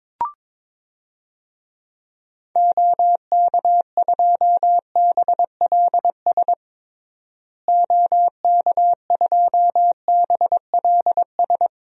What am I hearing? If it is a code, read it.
Morse code, heard as a keyed tone, about 22 wpm, 700 Hz. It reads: OK2BLS OK2BLS
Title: DXCC-100 Call Signs - 6 Characters - Repeated 2x - 22wpm